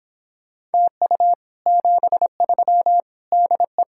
Morse code, heard as a keyed tone, about 31 wpm, 700 Hz. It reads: TU 73 DE